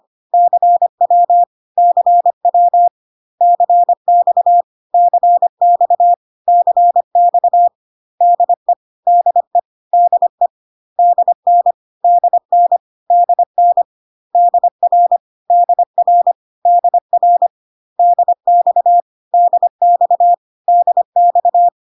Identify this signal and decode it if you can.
Morse code, heard as a keyed tone, about 25 wpm, 700 Hz. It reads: CW CW CX CX CX DE DE DE DN DN DN DR DR DR DX DX DX
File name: CW abbreviations alphabetical repeated 3 times 25 25wpm